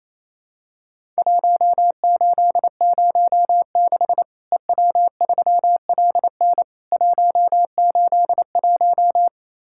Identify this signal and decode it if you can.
Morse code, heard as a keyed tone, about 28 wpm, 700 Hz. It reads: 1806 EW3LN 181
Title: Worked All Europe DX Contest - QTC Exchange - 28wpm